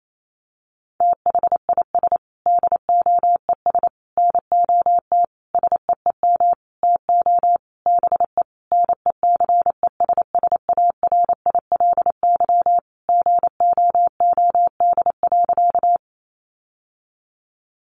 Morse code, heard as a keyed tone, about 28 wpm, 700 Hz. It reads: THIS DOES NOT SEEM TO BE NECESSARILY GOOD.
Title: Sentences from Top 1400 Words 28wpm